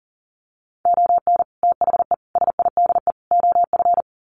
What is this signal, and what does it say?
Morse code, heard as a keyed tone, about 40 wpm, 700 Hz. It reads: ON THE SIDE OF
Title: Sets of 4 Words 40wpm